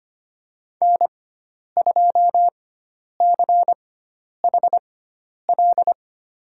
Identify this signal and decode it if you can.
Morse code, heard as a keyed tone, about 25 wpm, spaced as for 10 wpm, 700 Hz. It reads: N2CHL
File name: Lesson 16g - Letter c - Call Signs 25wpm